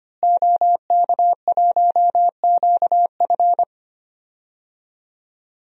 Morse code, heard as a keyed tone, about 25 wpm, 700 Hz. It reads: OK1QF
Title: DXCC-100 Call Signs - 5 Characters - Repeated 2x - 25wpm